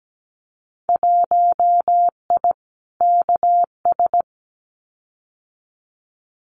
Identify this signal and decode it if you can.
Morse code, heard as a keyed tone, about 17 wpm, 700 Hz. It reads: 1I KS
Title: Winter Field Day Exchanges Encore 17wpm